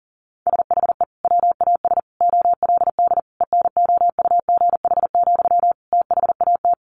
Morse code, heard as a keyed tone, about 40 wpm, 700 Hz. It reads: SHE WAS OLD ENOUGH, THAT
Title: Sentences from Top 800 Words 40wpm